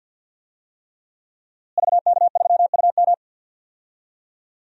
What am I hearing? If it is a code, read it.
Morse code, heard as a keyed tone, about 53 wpm, 700 Hz. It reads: VK3UM
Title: DXCC-100 Call Signs - 5 Characters - Repeated 3x - 50wpm